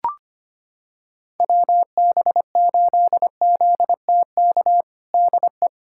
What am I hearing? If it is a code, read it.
Morse code, heard as a keyed tone, about 25 wpm, 700 Hz. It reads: WB8ZTK DE
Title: USA Callsign DE Callsign - 6 Characters - 25wpm